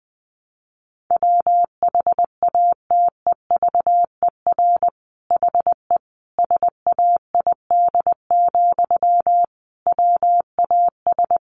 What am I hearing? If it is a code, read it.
Morse code, heard as a keyed tone, about 20 wpm, 700 Hz. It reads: WHATEVER HE SAID, WAS
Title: Sentences from Top 1100 Words 20wpm